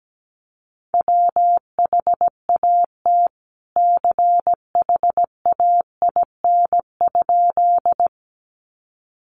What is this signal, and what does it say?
Morse code, heard as a keyed tone, about 17 wpm, 700 Hz. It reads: WHAT CHAIN?
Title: Sentences from Top 1500 Words 17wpm